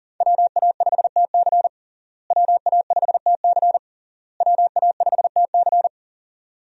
Morse code, heard as a keyed tone, about 40 wpm, 700 Hz. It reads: WA5TC WA5TC WA5TC
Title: USA Call Signs - 5 Characters - Repeated 3x - 40wpm